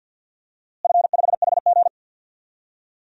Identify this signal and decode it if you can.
Morse code, heard as a keyed tone, about 52 wpm, 700 Hz. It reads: F5HG
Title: DXCC-100 Call Signs - 4 Characters - Repeated 3x - 50wpm